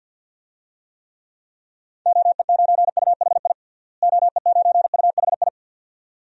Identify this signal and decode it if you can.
Morse code, heard as a keyed tone, about 46 wpm, 700 Hz. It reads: OE9UHI OE9UHI
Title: DXCC-100 Call Signs - 6 Characters - Repeated 2x - 50wpm